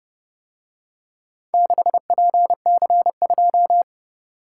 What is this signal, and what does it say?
Morse code, heard as a keyed tone, about 29 wpm, 700 Hz. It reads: 6PC2